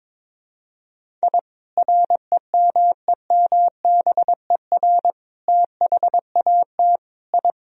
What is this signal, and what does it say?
Morse code, heard as a keyed tone, about 22 wpm, 700 Hz. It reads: I REMEMBER THAT I
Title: Sentences from Top 1000 Words - Review Part 1 - 22wpm